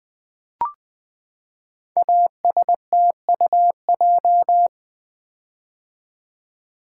Morse code, heard as a keyed tone, about 20 wpm, 700 Hz. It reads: ASTUJ
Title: Code Groups - 5 Letters - 20wpm